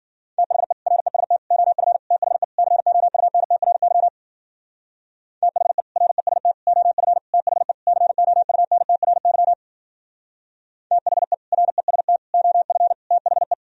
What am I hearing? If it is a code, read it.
Morse code, heard as a keyed tone, about 50 wpm, 700 Hz. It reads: THE REST OF THE COUNTRY THE REST OF THE COUNTRY THE REST OF THE